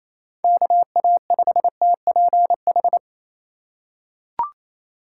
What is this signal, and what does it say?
Morse code, heard as a keyed tone, about 28 wpm, 700 Hz. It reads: KA5TPH
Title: USA Call Signs - 6 Characters - 2x Character Spacing - 28wpm